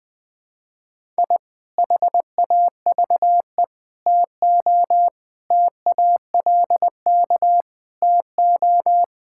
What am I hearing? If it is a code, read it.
Morse code, heard as a keyed tone, about 19 wpm, 700 Hz. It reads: I HAVE TO TALK TO
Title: Sentences with 3-Word Phrases - 20wpm